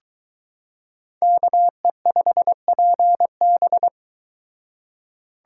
Morse code, heard as a keyed tone, about 22 wpm, 700 Hz. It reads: KE5PB